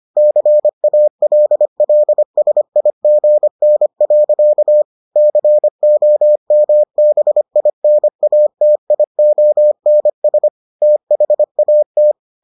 Morse code, heard as a keyed tone, about 25 wpm, 600 Hz. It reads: CALLSIGN. COMBINATIONS THAT